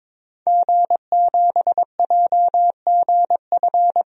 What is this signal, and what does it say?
Morse code, heard as a keyed tone, about 22 wpm, 700 Hz. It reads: G7JGF
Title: DXCC-100 Call Signs - 5 Characters - 22wpm